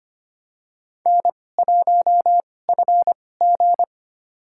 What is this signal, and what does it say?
Morse code, heard as a keyed tone, about 25 wpm, spaced as for 18 wpm, 700 Hz. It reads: N1FG
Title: USA Call Signs - 4 Characters - 2x Character Spacing - 25wpm